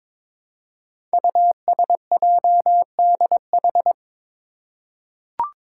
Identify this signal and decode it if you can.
Morse code, heard as a keyed tone, about 22 wpm, 700 Hz. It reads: USJDH
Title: Code Groups - 5 Letters - 22wpm